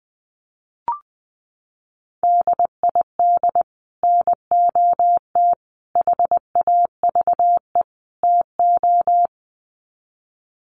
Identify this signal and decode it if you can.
Morse code, heard as a keyed tone, about 20 wpm, 700 Hz. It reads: DID NOT HAVE TO
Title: Sets of 4 Words 20wpm